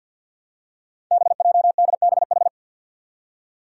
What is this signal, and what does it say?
Morse code, heard as a keyed tone, about 47 wpm, 700 Hz. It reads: BJDBH
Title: Five Challenging Letter-Number Combinations 50wpm